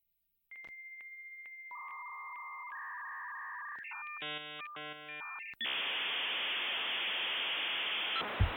Dial-up modem handshake.